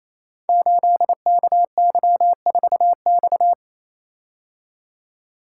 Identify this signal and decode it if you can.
Morse code, heard as a keyed tone, about 27 wpm, 700 Hz. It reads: 8KY4X
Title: Code Groups - 5 Letter:Numbers - 28wpm